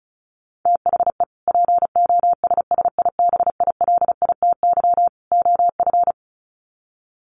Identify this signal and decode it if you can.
Morse code, heard as a keyed tone, about 34 wpm, 700 Hz. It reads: THE POSSIBILITY OF
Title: Sets of 3 Words 35wpm